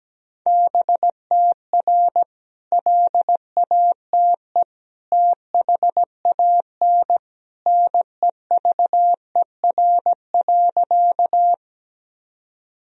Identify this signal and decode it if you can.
Morse code, heard as a keyed tone, about 17 wpm, 700 Hz. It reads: BTR LATE THAN NEVER.
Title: Sentences from Top 400 Words and Abbreviations 17wpm